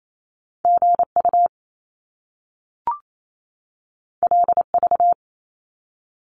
Morse code, heard as a keyed tone, about 28 wpm, 700 Hz. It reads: GU LV
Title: Top Two Letter Combinations in Words 28wpm